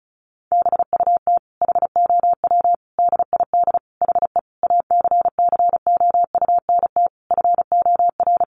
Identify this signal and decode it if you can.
Morse code, heard as a keyed tone, about 34 wpm, 700 Hz. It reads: BUT HOW DID HE ACCOUNT FOR
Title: Sentences from Top 900 Words 35wpm